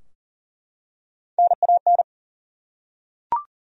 Morse code, heard as a keyed tone, about 40 wpm, 700 Hz. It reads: NAN
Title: USA Common Names - Max 3 Letters - 40wpm